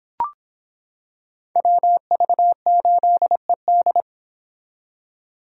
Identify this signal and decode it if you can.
Morse code, heard as a keyed tone, about 26 wpm, 700 Hz. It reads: WV8ED